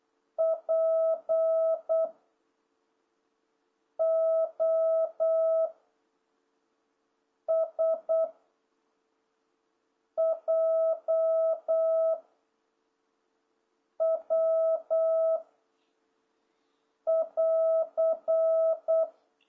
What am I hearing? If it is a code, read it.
Morse code, heard as a keyed tone, about 8 wpm, 660 Hz. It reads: P O S J W <AR>